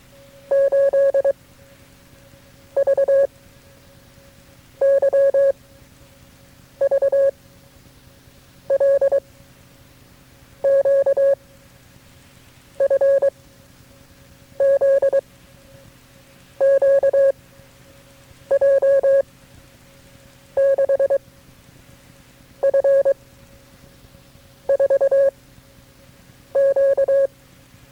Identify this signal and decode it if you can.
Morse code, heard as a keyed tone, about 23 wpm, 560 Hz. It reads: 8 V Y V L Q F Z Q J 6 F 4 Q
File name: FreeFiles / CW-practice / KB6MT-tape3-side1